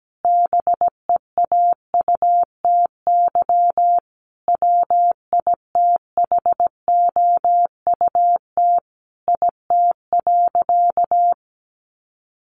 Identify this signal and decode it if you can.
Morse code, heard as a keyed tone, about 17 wpm, 700 Hz. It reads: BEAUTY WITHOUT IT.
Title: Sentences from Top 600 Words 17wpm